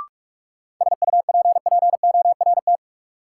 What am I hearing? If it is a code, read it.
Morse code, heard as a keyed tone, about 44 wpm, 700 Hz. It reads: SUPPORT
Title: Sentences from Top 800 Words - New Word Drill - 45wpm